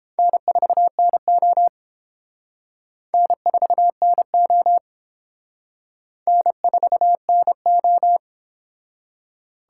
Morse code, heard as a keyed tone, about 32 wpm, 700 Hz. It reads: N4NO N4NO N4NO